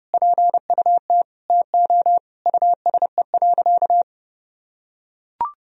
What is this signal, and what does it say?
Morse code, heard as a keyed tone, about 30 wpm, 700 Hz. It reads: PUT TO USE.
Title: Sentences from Top 1000 Words - Review Part 3 - 30wpm